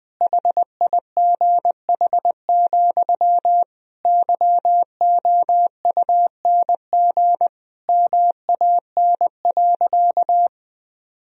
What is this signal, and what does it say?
Morse code, heard as a keyed tone, about 20 wpm, 700 Hz. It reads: HIGH, YOUNG MAN.
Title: Sentences from Top 1000 Words - Review Part 2 - 20wpm